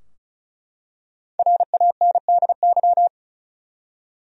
Morse code, heard as a keyed tone, about 35 wpm, 700 Hz. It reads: RANDY
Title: USA Common Names - Max 5 Letters - 35wpm